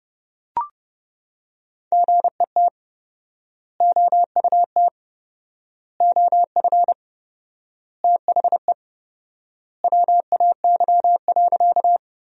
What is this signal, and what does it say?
Morse code, heard as a keyed tone, about 32 wpm, 700 Hz. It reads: GET OUT OF THE WAY.
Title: 4x Word-Spacing - Sentences from Top 200 Words - 30wpm